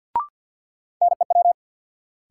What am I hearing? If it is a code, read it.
Morse code, heard as a keyed tone, about 52 wpm, 700 Hz. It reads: NEW
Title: Sentences from Top 300 Words - New Word Drill - 50wpm